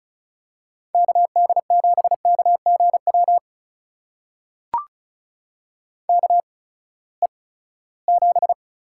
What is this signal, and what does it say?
Morse code, heard as a keyed tone, about 36 wpm, spaced as for 20 wpm, 700 Hz. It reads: KD7KGW KE7
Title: USA Call Signs - 6 Characters - 8x Character Spacing - 35wpm